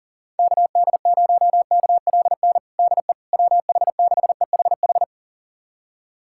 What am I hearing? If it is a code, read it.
Morse code, heard as a keyed tone, about 40 wpm, 700 Hz. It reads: KD0KLN DE WH6EHH
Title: USA Callsign DE Callsign - 6 Characters - 40wpm